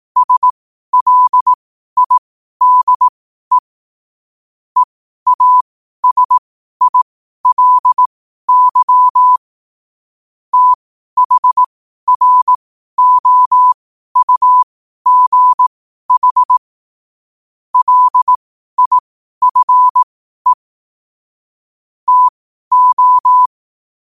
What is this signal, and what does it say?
Morse code, heard as a keyed tone, about 18 wpm, spaced as for 12 wpm, 1000 Hz. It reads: SLIDE EASILY THROUGH LIFE TO
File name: qotdfiles / 12wpm